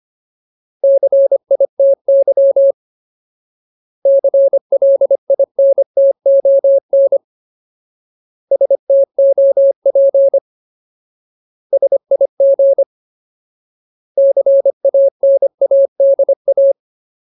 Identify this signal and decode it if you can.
Morse code, heard as a keyed tone, about 25 wpm, 550 Hz. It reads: CITY CLINTON STOP SIG CANADA
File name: Ultimate Word List / 25WPM Ultimate